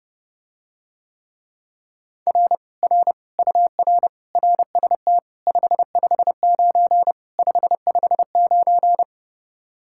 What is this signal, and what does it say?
Morse code, heard as a keyed tone, about 30 wpm, 700 Hz. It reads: R R UR RST 559 559